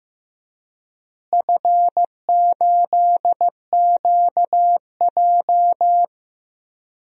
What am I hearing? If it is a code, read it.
Morse code, heard as a keyed tone, about 15 wpm, 700 Hz. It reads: F8QJ
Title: DXCC-100 Call Signs - 4 Characters - Repeated 3x - 15wpm